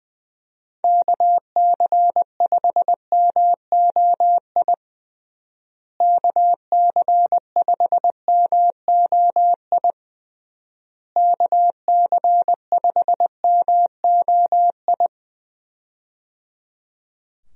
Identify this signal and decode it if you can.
Morse code, heard as a keyed tone, about 21 wpm, 700 Hz. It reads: KC5MOI KC5MOI KC5MOI
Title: USA Call Signs - 6 Characters - Repeated 3x - 20wpm